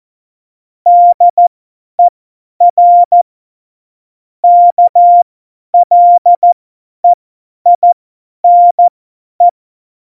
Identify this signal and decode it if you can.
Morse code, heard as a keyed tone, about 14 wpm, spaced as for 10 wpm, 700 Hz. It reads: DER KLEINE
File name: Morsekurs / Witze / Witz 02. Januar 2018, 50 F70